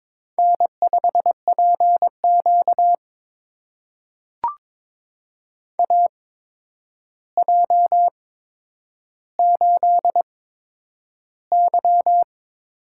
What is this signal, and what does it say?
Morse code, heard as a keyed tone, about 22 wpm, spaced as for 9 wpm, 700 Hz. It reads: N5PQ AJ8Y
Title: USA Call Signs - 4 Characters - 8x Character Spacing - 22wpm